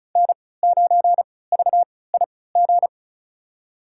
Morse code, heard as a keyed tone, about 36 wpm, spaced as for 20 wpm, 700 Hz. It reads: N9VIG
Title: USA Call Signs - 3x Character Spacing - 35wpm